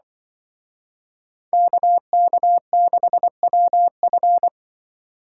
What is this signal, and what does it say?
Morse code, heard as a keyed tone, about 24 wpm, 700 Hz. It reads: KK6WF
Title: Speed Racing - USA Call Signs 20wpm